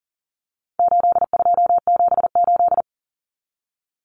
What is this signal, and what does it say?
Morse code, heard as a keyed tone, about 40 wpm, 700 Hz. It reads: 8278
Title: Numbers - 4-Digit 40wpm